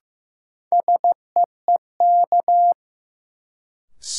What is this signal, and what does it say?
Morse code, heard as a keyed tone, about 15 wpm, 700 Hz. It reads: SEEK
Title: Sentences from Top 1600 Words - New Word Drill - 15wpm